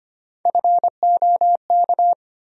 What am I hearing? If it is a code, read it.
Morse code, heard as a keyed tone, about 25 wpm, 700 Hz. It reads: FOK
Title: USA Call Sign Postfixes - 3 Letters 25wpm